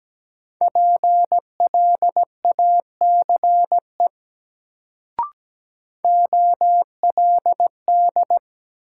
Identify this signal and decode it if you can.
Morse code, heard as a keyed tone, about 17 wpm, 700 Hz. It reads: PLACE OLD
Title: Sentences from Top 200 Words - New Word Drill - 17wpm